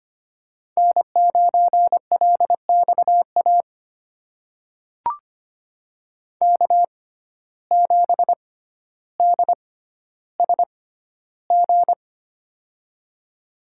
Morse code, heard as a keyed tone, about 25 wpm, spaced as for 13 wpm, 700 Hz. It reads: N9LXA K7DSG
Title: USA Call Signs - 5 Characters - 6x Character Spacing - 25wpm